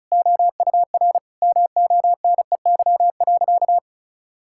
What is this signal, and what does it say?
Morse code, heard as a keyed tone, about 36 wpm, 690 Hz. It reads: OUR MONEY.
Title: Sentences from Top 1600 Words 35wpm